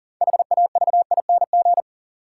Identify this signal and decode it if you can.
Morse code, heard as a keyed tone, about 40 wpm, 700 Hz. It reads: HAVING
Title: Sets of 5 Words Encore - New Word Drill - 40wpm